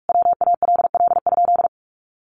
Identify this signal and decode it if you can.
Morse code, heard as a keyed tone, about 44 wpm, 700 Hz. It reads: WALL?
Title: Sentences from Top 1100 Words 45wpm